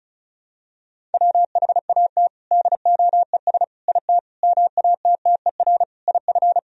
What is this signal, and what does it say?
Morse code, heard as a keyed tone, about 36 wpm, 700 Hz. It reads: WHAT DOES IT MATTER IF
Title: Sentences from Top 1000 Words - Review Part 1 - 35wpm